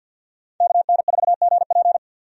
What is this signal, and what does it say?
Morse code, heard as a keyed tone, about 52 wpm, 700 Hz. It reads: KN4GP
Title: USA Call Signs - 5 Characters - 50wpm